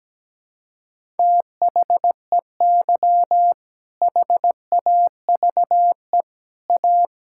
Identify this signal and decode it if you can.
Morse code, heard as a keyed tone, about 17 wpm, 700 Hz. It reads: THEY HAVE A
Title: Sentences from Top 1000 Words - Review Part 5 - 17wpm